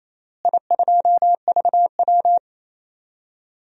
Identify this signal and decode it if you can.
Morse code, heard as a keyed tone, about 28 wpm, 700 Hz. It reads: I2VW
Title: DXCC-100 Call Signs - 4 Characters - Repeated 3x - 28wpm